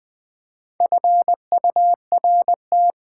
Morse code, heard as a keyed tone, about 20 wpm, 700 Hz. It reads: FURT
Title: Build A Word - 4 Letters At A Time - 11-Letter Words - 20wpm